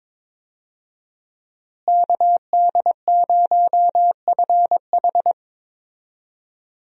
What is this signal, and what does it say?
Morse code, heard as a keyed tone, about 22 wpm, 700 Hz. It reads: KD0FH
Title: USA Call Signs - 5 Characters - Repeated 2x - 22wpm